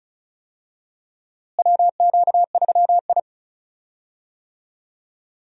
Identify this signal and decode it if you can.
Morse code, heard as a keyed tone, about 34 wpm, 700 Hz. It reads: WQ3I
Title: USA Call Signs - 4 Characters - Repeated 2x - 35wpm